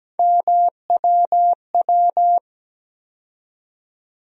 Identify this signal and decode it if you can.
Morse code, heard as a keyed tone, about 17 wpm, 700 Hz. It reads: MWW
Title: USA Call Sign Postfixes - 3 Letters 17wpm